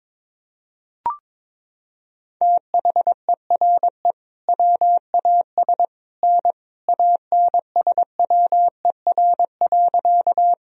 Morse code, heard as a keyed tone, about 22 wpm, 700 Hz. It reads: THERE WAS N ANSWER.
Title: Sentences from Top 300 Words and Abbreviations 22wpm